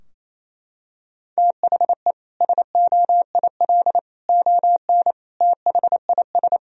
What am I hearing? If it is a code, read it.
Morse code, heard as a keyed tone, about 28 wpm, 700 Hz. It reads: THE SOIL ON THIS
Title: Sentences from Top 1000 Words - Review Part 4 - 28wpm